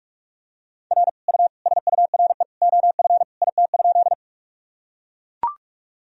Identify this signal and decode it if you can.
Morse code, heard as a keyed tone, about 41 wpm, 700 Hz. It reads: R U SURE OF IT?
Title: Sentences from Top 300 Words and Abbreviations 45wpm